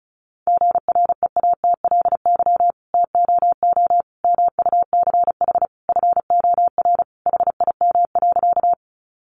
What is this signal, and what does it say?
Morse code, heard as a keyed tone, about 34 wpm, 700 Hz. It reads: GREATLY TOO MUCH FOR HIM.
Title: Sentences from Top 1400 Words 35wpm